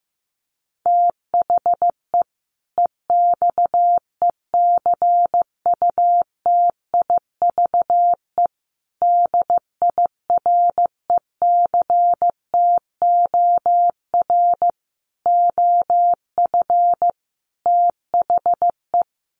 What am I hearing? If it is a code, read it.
Morse code, heard as a keyed tone, about 15 wpm, 700 Hz. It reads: THE EXECUTIVE DIRECTOR OF THE
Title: Sets of 5 Words Encore 15wpm